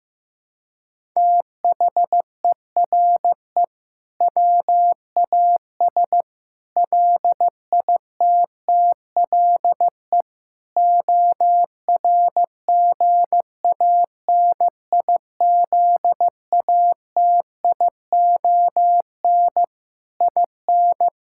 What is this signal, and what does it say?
Morse code, heard as a keyed tone, about 15 wpm, 700 Hz. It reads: THERE WAS LITTLE ORGANIZATION IN
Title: Sentences from Top 1000 Words - Review Part 5 - 15wpm